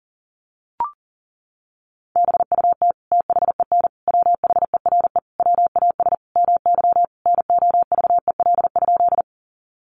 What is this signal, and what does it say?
Morse code, heard as a keyed tone, about 40 wpm, 700 Hz. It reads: BUT THEN WHERE WAS MY NOVEL?
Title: Sentences from Top 1300 Words 40wpm